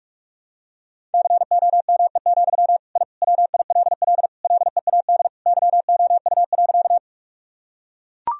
Morse code, heard as a keyed tone, about 42 wpm, 700 Hz. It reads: COME, I WILL LEAD YOU.
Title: Sentences from Top 1000 Words - Review Part 2 - 45wpm